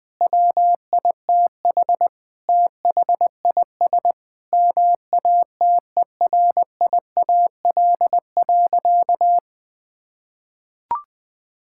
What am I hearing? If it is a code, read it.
Morse code, heard as a keyed tone, about 20 wpm, 700 Hz. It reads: WITH THIS MATERIAL.
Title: Sentences from Top 1000 Words - Review Part 5 - 20wpm